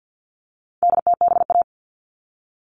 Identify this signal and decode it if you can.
Morse code, heard as a keyed tone, about 50 wpm, 700 Hz. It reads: DTBA
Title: All Cut Numbers - 4-Digit 50wpm